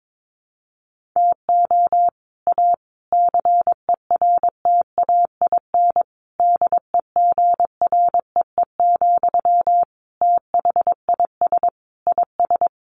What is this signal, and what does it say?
Morse code, heard as a keyed tone, about 22 wpm, 700 Hz. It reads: TO A CERTAIN DEGREE, THIS IS
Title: Sentences from Top 1500 Words 22wpm